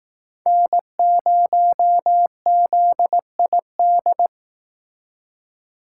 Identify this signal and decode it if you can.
Morse code, heard as a keyed tone, about 18 wpm, 700 Hz. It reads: N0ZID
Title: Speed Racing - USA 1x3 Call Signs 15wpm